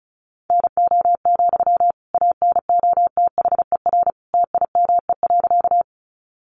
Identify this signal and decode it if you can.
Morse code, heard as a keyed tone, about 34 wpm, 700 Hz. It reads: NO, ANOTHER TIME.
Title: Sentences from Top 600 Words 35wpm